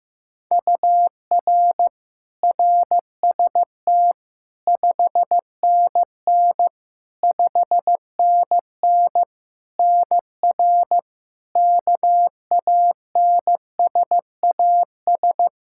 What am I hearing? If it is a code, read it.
Morse code, heard as a keyed tone, about 15 wpm, 700 Hz. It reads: UR RST 5NN 5NN NR KANSAS
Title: Practice QSO - Exchange 1 - Beginning - 15wpm